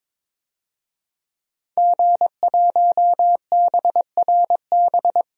Morse code, heard as a keyed tone, about 22 wpm, 700 Hz. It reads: G1BRB
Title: DXCC-100 Call Signs - 5 Characters - 22wpm